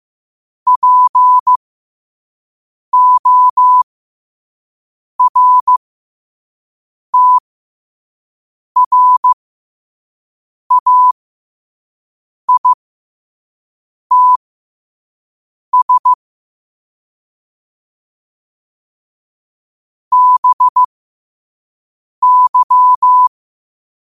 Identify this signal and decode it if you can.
Morse code, heard as a keyed tone, about 15 wpm, spaced as for 5 wpm, 1000 Hz. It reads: PORTRAITS BY
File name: qotdfiles / 5wpm